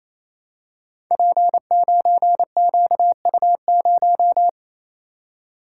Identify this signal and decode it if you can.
Morse code, heard as a keyed tone, about 27 wpm, 700 Hz. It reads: P9QU0
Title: Code Groups - 5 Letter:Numbers - 28wpm